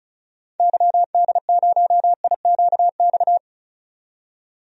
Morse code, heard as a keyed tone, about 34 wpm, 700 Hz. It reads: YD0IQX
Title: DXCC-100 Call Signs - 6 Characters - 35wpm